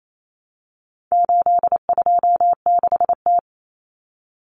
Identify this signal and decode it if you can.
Morse code, heard as a keyed tone, about 28 wpm, 700 Hz. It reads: 826T
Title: Common Cut Numbers - 4-Digit 28wpm